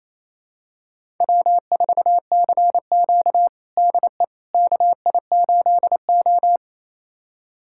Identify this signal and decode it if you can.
Morse code, heard as a keyed tone, about 28 wpm, 700 Hz. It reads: W4CQ DE KI8O
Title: USA Callsign DE Callsign - 4 Characters - 28wpm